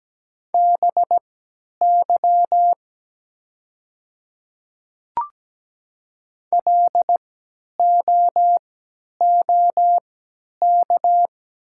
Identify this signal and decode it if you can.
Morse code, heard as a keyed tone, about 17 wpm, spaced as for 10 wpm, 700 Hz. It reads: BY LOOK
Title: 3x Character Spacing - Top 100 Words - 17wpm